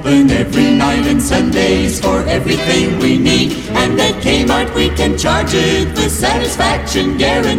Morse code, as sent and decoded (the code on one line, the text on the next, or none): none